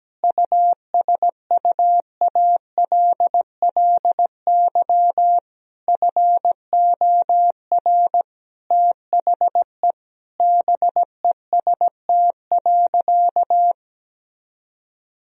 Morse code, heard as a keyed tone, about 17 wpm, 700 Hz. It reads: USUALLY FOR THE BEST.
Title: Sentences from Top 1000 Words - Review Part 3 - 17wpm